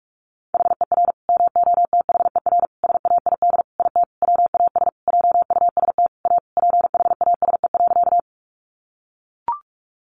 Morse code, heard as a keyed tone, about 44 wpm, 700 Hz. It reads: HER MOTHER SAID IT WAS JUST A PHASE.
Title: Sentences from Top 1200 Words 45wpm